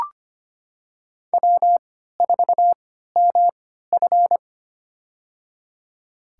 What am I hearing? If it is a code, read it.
Morse code, heard as a keyed tone, about 25 wpm, spaced as for 14 wpm, 700 Hz. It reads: W4MF
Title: USA Call Signs - 4 Characters - 3x Character Spacing - 25wpm